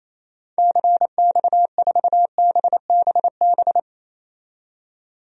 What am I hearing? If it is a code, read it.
Morse code, heard as a keyed tone, about 28 wpm, 700 Hz. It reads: CX4BBB
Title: DXCC-100 Call Signs - 6 Characters - 28wpm